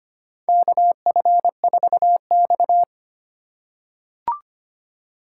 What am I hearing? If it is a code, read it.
Morse code, heard as a keyed tone, about 25 wpm, 700 Hz. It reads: KF4X
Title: USA Call Signs - 4 Characters - 4x Character Spacing - 25wpm